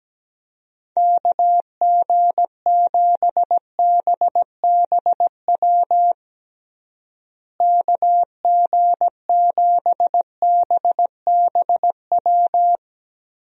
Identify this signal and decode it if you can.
Morse code, heard as a keyed tone, about 17 wpm, 700 Hz. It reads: KG7BBW KG7BBW
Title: USA Call Signs - 6 Characters - Repeated 3x - 17wpm